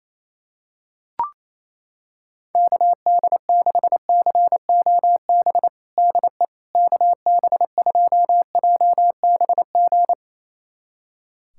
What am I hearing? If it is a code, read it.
Morse code, heard as a keyed tone, about 28 wpm, 700 Hz. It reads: KD6COB DE KB2JBG
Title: USA Callsign DE Callsign - 6 Characters - 28wpm